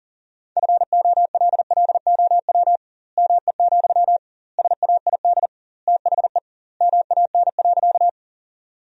Morse code, heard as a keyed tone, about 37 wpm, 700 Hz. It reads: FOLLOW ME, SAID THE MAN.
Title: 2x Word-Spacing - Sentences from Top 200 Words - 40wpm